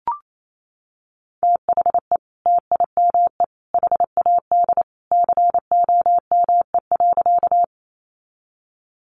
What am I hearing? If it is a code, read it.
Morse code, heard as a keyed tone, about 28 wpm, 700 Hz. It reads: THE TIME HAD COME.
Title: Sentences from Top 100 Words 28wpm